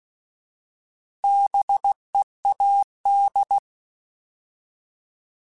Morse code, heard as a keyed tone, about 16 wpm, 800 Hz. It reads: BEAD
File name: Alphabetical / Alphabetical mp3s / Bs @15